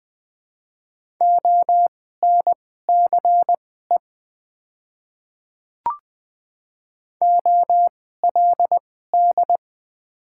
Morse code, heard as a keyed tone, about 21 wpm, spaced as for 15 wpm, 700 Hz. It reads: ONCE OLD
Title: 2x Character Spacing - Top 300 Words - 20wpm